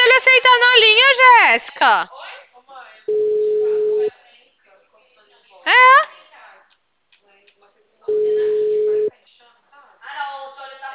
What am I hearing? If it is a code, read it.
Ringback tone (Europe, 425 Hz), with speech.
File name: out- -2004- - - .4887